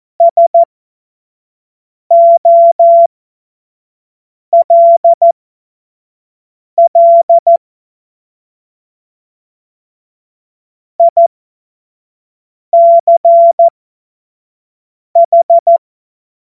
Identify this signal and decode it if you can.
Morse code, heard as a keyed tone, about 14 wpm, spaced as for 5 wpm, 680 Hz. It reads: SOLL ICH
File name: Morsekurs / Witze / Witz 12 Dezember 2023 25 F70